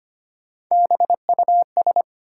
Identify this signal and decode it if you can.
Morse code, heard as a keyed tone, about 25 wpm, 700 Hz. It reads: BUS
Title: Sentences from Top 1000 Words - Review Part 6 - New Word Drill - 25wpm